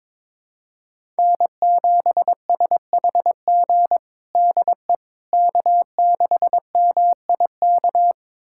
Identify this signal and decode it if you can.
Morse code, heard as a keyed tone, about 21 wpm, 700 Hz. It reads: N7SHG DE K6MIK
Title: USA Callsign DE Callsign - 5 Characters - 22wpm